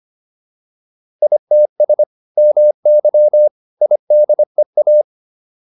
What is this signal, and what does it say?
Morse code, heard as a keyed tone, about 25 wpm, 600 Hz. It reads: ITS MY IDEA
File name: PR302 25